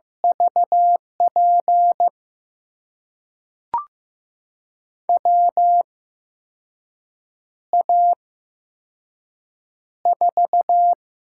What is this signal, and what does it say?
Morse code, heard as a keyed tone, about 15 wpm, spaced as for 5 wpm, 700 Hz. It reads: VP WA4
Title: USA Call Signs - 6 Characters - 8x Character Spacing - 15wpm